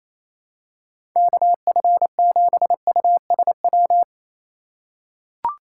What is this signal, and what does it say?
Morse code, heard as a keyed tone, about 28 wpm, 700 Hz. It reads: KF7USW
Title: USA Call Signs - 6 Characters - 3x Character Spacing - 28wpm